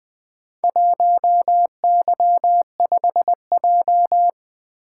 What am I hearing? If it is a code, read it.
Morse code, heard as a keyed tone, about 20 wpm, 700 Hz. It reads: 1Y5J
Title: Four Letter-Number Combinations - 20wpm